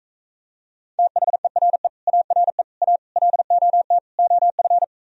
Morse code, heard as a keyed tone, about 36 wpm, 700 Hz. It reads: THERE ARE A LOT OF